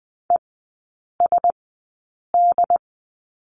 Morse code, heard as a keyed tone, about 20 wpm, spaced as for 8 wpm, 700 Hz. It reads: ESD
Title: 074 Lesson 13e - Letter L - Three Characters 20wpm